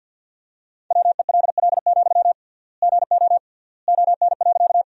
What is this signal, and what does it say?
Morse code, heard as a keyed tone, about 46 wpm, 700 Hz. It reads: WELL, GO ON.
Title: 3x Word-Spacing - Sentences from Top 200 Words - 50wpm